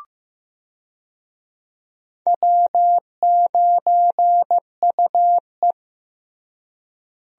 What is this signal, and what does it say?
Morse code, heard as a keyed tone, about 15 wpm, 700 Hz. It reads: W9UE
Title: USA Call Signs - 4 Characters - Repeated 3x - 15wpm